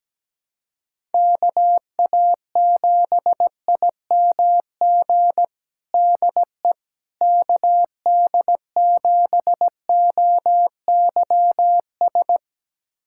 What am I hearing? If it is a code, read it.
Morse code, heard as a keyed tone, about 17 wpm, 700 Hz. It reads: KA7IMG DE KD7OYS
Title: USA Callsign DE Callsign - 6 Characters - 17wpm